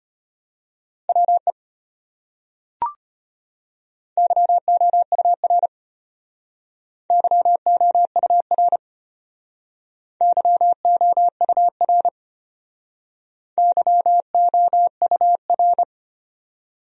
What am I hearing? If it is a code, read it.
Morse code, heard as a keyed tone, about 33 wpm, 700 Hz. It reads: WE YOUR YOUR YOUR YOUR